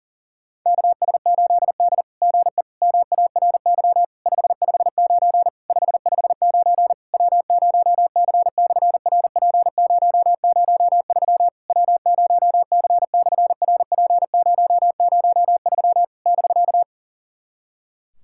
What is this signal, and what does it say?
Morse code, heard as a keyed tone, about 40 wpm, 700 Hz. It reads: KS8D GE MARY 559 559 W0C/RP003 W0C/RP003 <BK>
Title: SOTA QSO Practice 40wpm